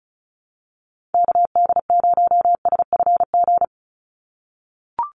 Morse code, heard as a keyed tone, about 34 wpm, 700 Hz. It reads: KD0SFG
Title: USA Call Signs 35wpm